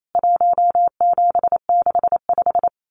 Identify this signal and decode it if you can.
Morse code, heard as a keyed tone, about 28 wpm, 700 Hz. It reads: 1765
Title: CWT Exchange 28wpm